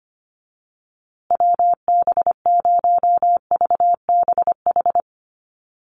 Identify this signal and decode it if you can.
Morse code, heard as a keyed tone, about 25 wpm, 700 Hz. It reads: WB0VBH
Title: USA Call Signs Repeated 3x 25wpm